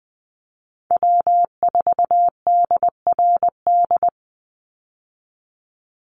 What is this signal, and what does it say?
Morse code, heard as a keyed tone, about 20 wpm, 700 Hz. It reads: W4DRD
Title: USA Call Signs 20wpm